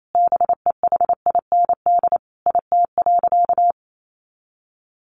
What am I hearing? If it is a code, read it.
Morse code, heard as a keyed tone, about 28 wpm, 700 Hz. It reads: BEHIND IT.
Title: Sentences from Top 500 Words 28wpm